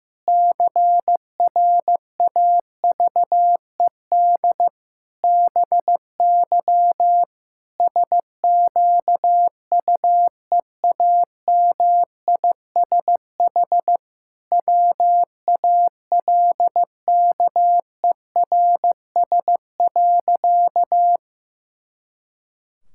Morse code, heard as a keyed tone, about 15 wpm, 700 Hz. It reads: CRAVED BY SQUEAMISH WALKERS.
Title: Pangram Challenge v2 - 15wpm